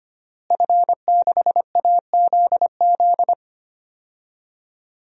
Morse code, heard as a keyed tone, about 25 wpm, 700 Hz. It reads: F6AZZ
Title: DXCC-100 Call Signs - 5 Characters - Repeated 3x - 25wpm